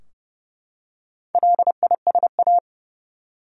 Morse code, heard as a keyed tone, about 32 wpm, 700 Hz. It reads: LISA
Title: USA Common Names - Max 4 Letters - 30wpm